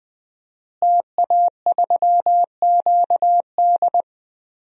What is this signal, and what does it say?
Morse code, heard as a keyed tone, about 21 wpm, 700 Hz. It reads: TA3QD
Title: DXCC-100 Call Signs - 5 Characters - Repeated 3x - 20wpm